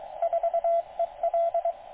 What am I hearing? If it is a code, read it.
Morse code, heard as a keyed tone, about 23 wpm, 680 Hz, with radio noise. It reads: 4EL